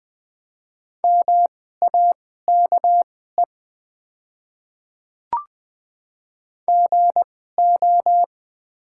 Morse code, heard as a keyed tone, about 20 wpm, spaced as for 14 wpm, 700 Hz. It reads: MAKE GO
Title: 2x Character Spacing - Top 300 Words - 20wpm